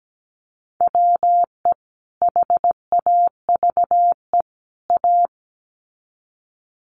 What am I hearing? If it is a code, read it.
Morse code, heard as a keyed tone, about 17 wpm, 700 Hz. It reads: WE HAVE A
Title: Sets of 3 Words 17wpm